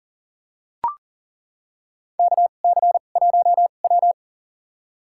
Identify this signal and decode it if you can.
Morse code, heard as a keyed tone, about 38 wpm, spaced as for 28 wpm, 700 Hz. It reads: KC1W
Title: USA Call Signs - 2x Character Spacing - 40wpm